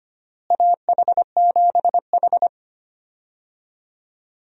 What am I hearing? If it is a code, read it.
Morse code, heard as a keyed tone, about 25 wpm, 700 Hz. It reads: AH7H
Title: USA Call Signs - 4 Characters - Repeated 3x - 25wpm